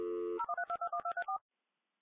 DTMF, heard as telephone touch tones, tones of about 90 ms. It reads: 013221234